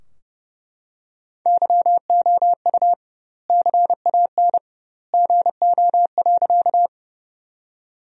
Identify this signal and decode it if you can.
Morse code, heard as a keyed tone, about 32 wpm, 700 Hz. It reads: YOU CAN GO.
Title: 2x Word-Spacing - Sentences from Top 100 Words - 30wpm